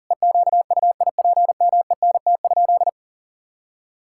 Morse code, heard as a keyed tone, about 40 wpm, 700 Hz. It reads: EQUIPMENT?
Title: Sentences from Top 1000 Words - Review Part 5 - 40wpm